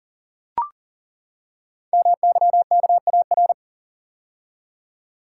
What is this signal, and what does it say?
Morse code, heard as a keyed tone, about 42 wpm, 690 Hz. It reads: MYKAR
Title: Code Groups - 5 Letters - 40wpm